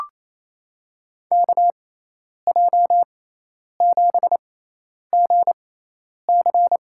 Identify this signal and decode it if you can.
Morse code, heard as a keyed tone, about 28 wpm, spaced as for 10 wpm, 700 Hz. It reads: KJ7GC
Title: USA Call Signs - 6x Character Spacing - 28wpm